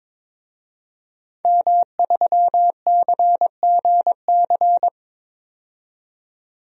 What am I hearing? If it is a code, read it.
Morse code, heard as a keyed tone, about 22 wpm, 700 Hz. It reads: M3CGC
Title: DXCC-100 Call Signs - 5 Characters - 22wpm